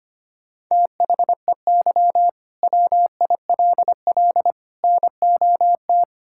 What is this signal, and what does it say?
Morse code, heard as a keyed tone, about 25 wpm, 700 Hz. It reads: THEY WILL NOT
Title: Sentences from Top 1000 Words - Review Part 5 - 25wpm